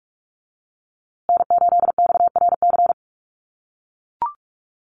Morse code, heard as a keyed tone, about 44 wpm, 700 Hz. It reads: N8XRC
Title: USA Call Signs Repeated 3x 45wpm